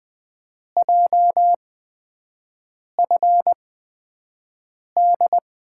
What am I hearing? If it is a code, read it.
Morse code, heard as a keyed tone, about 20 wpm, spaced as for 5 wpm, 700 Hz. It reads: JFD
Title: USA Call Signs - 5 Characters - 8x Character Spacing - 20wpm